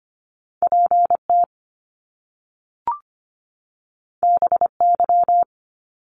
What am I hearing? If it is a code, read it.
Morse code, heard as a keyed tone, about 25 wpm, 700 Hz. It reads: PT BY
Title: Top Two Letter Combinations in Words 25wpm